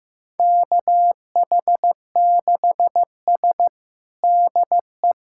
Morse code, heard as a keyed tone, about 15 wpm, 700 Hz. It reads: KH6S DE
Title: USA Callsign DE Callsign - 4 Characters - 15wpm